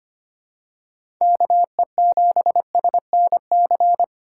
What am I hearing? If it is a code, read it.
Morse code, heard as a keyed tone, about 25 wpm, 700 Hz. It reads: KE7SNC
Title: USA Call Signs - 4x Character Spacing - 25wpm